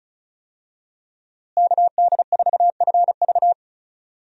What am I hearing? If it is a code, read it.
Morse code, heard as a keyed tone, about 34 wpm, 700 Hz. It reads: KD4FV
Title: USA Call Signs - 5 Characters - Repeated 2x - 35wpm